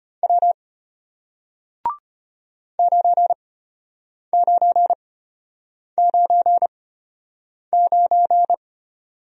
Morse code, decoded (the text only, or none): W 9999